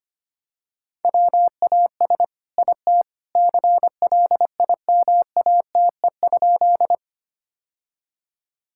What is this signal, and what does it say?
Morse code, heard as a keyed tone, about 25 wpm, 700 Hz. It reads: WAS IT CLIMATE?
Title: Sentences from Top 1600 Words 25wpm